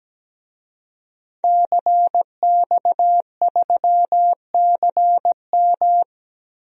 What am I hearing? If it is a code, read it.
Morse code, heard as a keyed tone, about 17 wpm, 700 Hz. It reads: CX3CM
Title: DXCC-100 Call Signs - 5 Characters - Repeated 3x - 17wpm